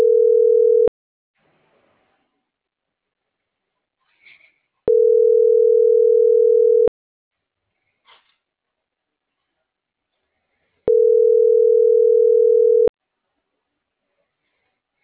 Ringback tone (North America), 440 + 480 Hz.